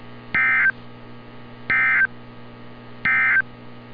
Emergency Alert System (SAME) tones, heard as harsh buzzing bursts. It is heard from 0.2 to 3.5 s.
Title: NOAA Weather Radio - Altoona, PA